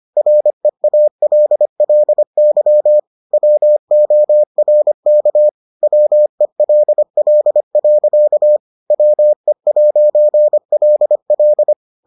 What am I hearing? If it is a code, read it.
Morse code, heard as a keyed tone, about 25 wpm, 600 Hz. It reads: REALLY WORK WELL. WE'LL